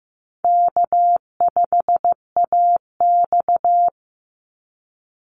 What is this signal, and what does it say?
Morse code, heard as a keyed tone, about 15 wpm, 700 Hz. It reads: K5AX
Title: Easy Call Signs - 15wpm